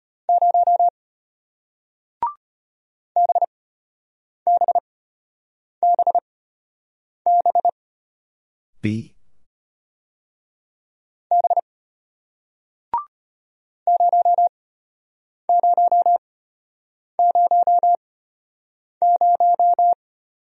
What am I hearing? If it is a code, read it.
Morse code, heard as a keyed tone, about 33 wpm, spaced as for 8 wpm, 700 Hz. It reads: 0 BBBB B 0000